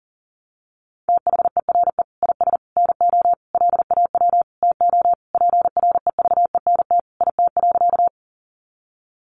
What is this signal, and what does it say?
Morse code, heard as a keyed tone, about 40 wpm, 700 Hz. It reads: THERE IS NO LAW TO PREVENT IT.
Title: Sentences from Top 1100 Words 40wpm